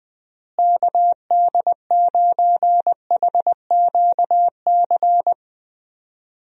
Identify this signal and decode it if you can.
Morse code, heard as a keyed tone, about 20 wpm, 700 Hz. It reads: KD9HQC
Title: USA Call Signs - 6 Characters - Repeated 3x - 20wpm